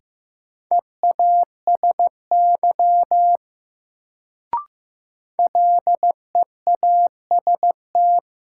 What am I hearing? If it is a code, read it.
Morse code, heard as a keyed tone, about 15 wpm, 700 Hz. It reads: EASY LEAST
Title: Sentences from Top 700 Words - New Word Drill - 15wpm